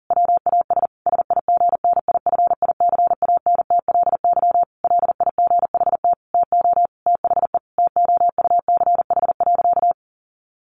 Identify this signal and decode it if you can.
Morse code, heard as a keyed tone, about 40 wpm, 700 Hz. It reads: WAS SIGNIFICANTLY LIGHT TO THE TOUCH.
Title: Sentences from Top 1500 Words 40wpm